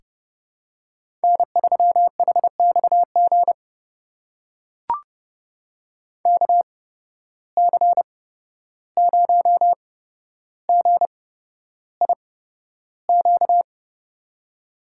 Morse code, heard as a keyed tone, about 30 wpm, spaced as for 12 wpm, 700 Hz. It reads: N3HXG KC0GIQ
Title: USA Call Signs - 8x Character Spacing - 30wpm